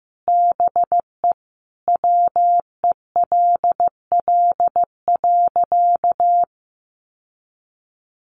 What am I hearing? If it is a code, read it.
Morse code, heard as a keyed tone, about 14 wpm, 700 Hz. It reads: BE WELL.
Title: Sentences from Top 1500 Words 15wpm